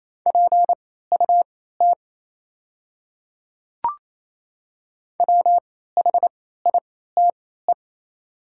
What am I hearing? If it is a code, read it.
Morse code, heard as a keyed tone, about 28 wpm, spaced as for 16 wpm, 690 Hz. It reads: PUT WHITE
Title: 3x Character Spacing - Top 500 Words - 28wpm